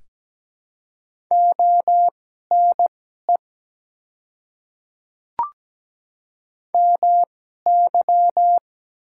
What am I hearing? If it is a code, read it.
Morse code, heard as a keyed tone, about 17 wpm, spaced as for 12 wpm, 700 Hz. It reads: ONE MY
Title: 2x Character Spacing - Top 200 Words - 17wpm